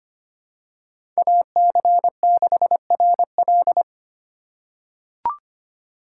Morse code, heard as a keyed tone, about 25 wpm, 700 Hz. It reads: AC6RL
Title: USA Call Signs - 5 Characters - 2x Character Spacing - 25wpm